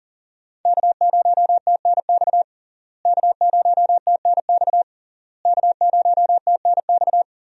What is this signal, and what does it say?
Morse code, heard as a keyed tone, about 37 wpm, 700 Hz. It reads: K0TNX K0TNX K0TNX
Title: USA Call Signs - 5 Characters - Repeated 3x - 40wpm